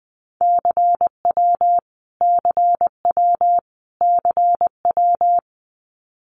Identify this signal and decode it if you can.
Morse code, heard as a keyed tone, about 20 wpm, 700 Hz. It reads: CW CW CW
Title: Abbreviations Repeated 3x - 20wpm